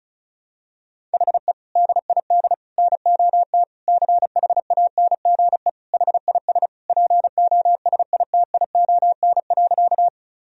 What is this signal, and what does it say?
Morse code, heard as a keyed tone, about 36 wpm, 700 Hz. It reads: HE DID NOT CHANGE HIS POSITION.
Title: Sentences from Top 1000 Words - Review Part 1 - 35wpm